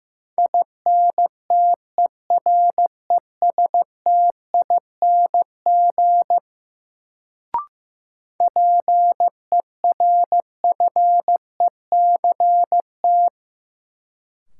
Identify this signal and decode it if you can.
Morse code, heard as a keyed tone, about 15 wpm, 700 Hz. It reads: INTERESTING PERFECT
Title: Sentences from Top 1000 Words - New Word Drill - 15wpm